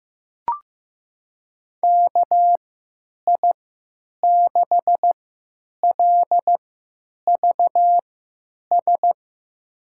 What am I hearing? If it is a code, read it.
Morse code, heard as a keyed tone, about 16 wpm, spaced as for 9 wpm, 700 Hz. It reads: KI6LVS
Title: USA Call Signs - 3x Character Spacing - 15wpm